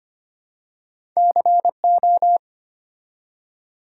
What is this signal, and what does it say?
Morse code, heard as a keyed tone, about 25 wpm, 700 Hz. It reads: CO
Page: morse-code-ninja-practice-25wpm